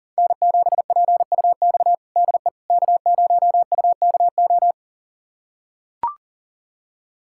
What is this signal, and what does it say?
Morse code, heard as a keyed tone, about 37 wpm, 700 Hz. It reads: N7PUX DE K0UKO
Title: USA Callsign DE Callsign - 5 Characters - 40wpm